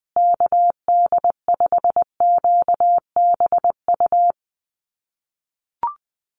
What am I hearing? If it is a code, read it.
Morse code, heard as a keyed tone, about 20 wpm, 700 Hz. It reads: KD5QBU
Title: USA Call Signs Repeated 3x 20wpm